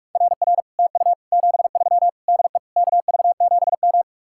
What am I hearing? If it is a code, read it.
Morse code, heard as a keyed tone, about 44 wpm, 700 Hz. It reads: RR TU 73 DE KV7M